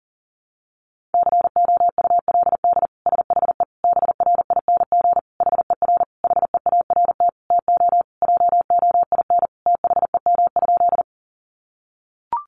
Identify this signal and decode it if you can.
Morse code, heard as a keyed tone, about 40 wpm, 700 Hz. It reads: COULD SHE BRING HER HEART TO JOIN THEM?
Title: Sentences from Top 1400 Words 40wpm